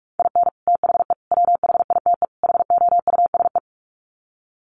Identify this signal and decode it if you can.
Morse code, heard as a keyed tone, about 44 wpm, 700 Hz. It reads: IN THE WHITE HOUSE
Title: Sets of 4 Words 45wpm